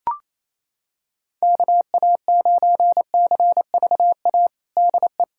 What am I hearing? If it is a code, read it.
Morse code, heard as a keyed tone, about 28 wpm, 700 Hz. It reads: KA9CVA DE
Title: USA Callsign DE Callsign 28wpm